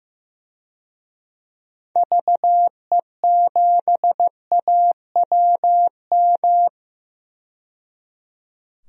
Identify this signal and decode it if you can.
Morse code, heard as a keyed tone, about 15 wpm, 700 Hz. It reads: VE7AWM
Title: DXCC-100 Call Signs - 6 Characters - 15wpm